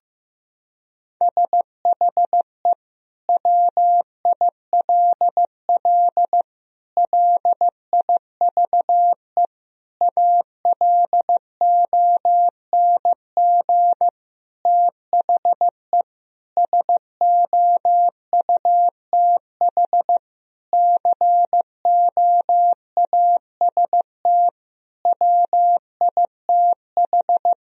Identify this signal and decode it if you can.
Morse code, heard as a keyed tone, about 15 wpm, 700 Hz. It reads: SHE WILL LIVE ALONG THE SOUTH COAST WITH
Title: Sentences from Top 1000 Words - Review Part 4 - 15wpm